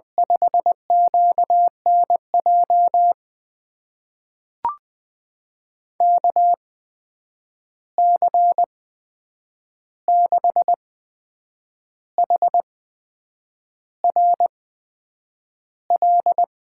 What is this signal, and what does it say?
Morse code, heard as a keyed tone, about 20 wpm, spaced as for 8 wpm, 700 Hz. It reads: 5QNJ KC6HRL
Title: USA Call Signs - 6 Characters - 8x Character Spacing - 20wpm